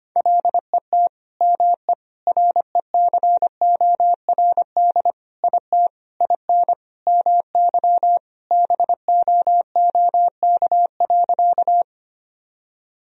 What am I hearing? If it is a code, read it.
Morse code, heard as a keyed tone, about 25 wpm, 700 Hz. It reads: LET ME RECORD IT IN MY BOOK.
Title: Sentences from Top 1000 Words - Review Part 3 - 25wpm